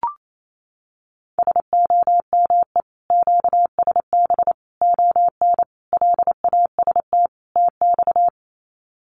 Morse code, heard as a keyed tone, about 28 wpm, 700 Hz. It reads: SOME QSB ON LAST TX
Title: Common QSO Phrases 28wpm